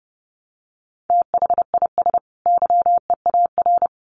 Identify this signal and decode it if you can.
Morse code, heard as a keyed tone, about 30 wpm, 700 Hz. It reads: THIS YEAR
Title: Sets of 2 Words 30wpm